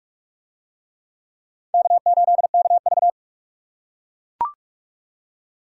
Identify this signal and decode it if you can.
Morse code, heard as a keyed tone, about 40 wpm, 700 Hz. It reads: K8KV